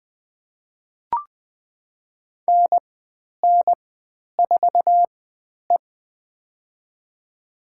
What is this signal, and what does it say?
Morse code, heard as a keyed tone, about 20 wpm, spaced as for 10 wpm, 700 Hz. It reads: NN4E
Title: 095 Lesson 16g - Letter c - Call Signs 20wpm